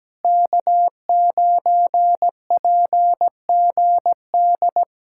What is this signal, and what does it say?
Morse code, heard as a keyed tone, about 17 wpm, 700 Hz. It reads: K9PGD
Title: Speed-Racing Common QSO Phrases - 17wpm